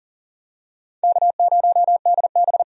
Morse code, heard as a keyed tone, about 40 wpm, 700 Hz. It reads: K0DB
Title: USA Call Signs - 4 Characters - 40wpm